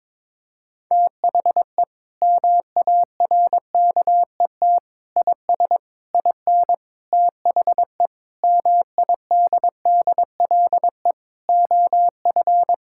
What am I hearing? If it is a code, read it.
Morse code, heard as a keyed tone, about 22 wpm, 700 Hz. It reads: THE MARKET IS IN THE MIDDLE OF
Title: Sentences from Top 1000 Words - Review Part 6 - 22wpm